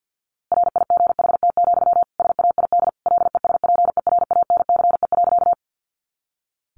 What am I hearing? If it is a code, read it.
Morse code, heard as a keyed tone, about 50 wpm, 700 Hz. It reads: RIGHT, SAID LESPERANCE.
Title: A Sound Of Thunder 50wpm